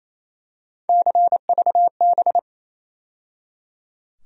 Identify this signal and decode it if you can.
Morse code, heard as a keyed tone, about 28 wpm, 700 Hz. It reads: CVB
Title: USA Call Sign Postfixes - 3 Letters 28wpm